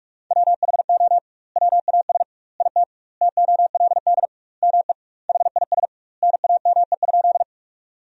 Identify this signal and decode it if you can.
Morse code, heard as a keyed tone, about 44 wpm, 700 Hz. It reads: WHO WAS IT TOLD ME HIS NAME?
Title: 2x Word-Spacing - Sentences from Top 400 Words - 45wpm